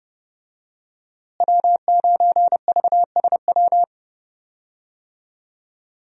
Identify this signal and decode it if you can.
Morse code, heard as a keyed tone, about 30 wpm, 700 Hz. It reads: W9VSW
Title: USA Call Signs - 5 Characters - Repeated 3x - 30wpm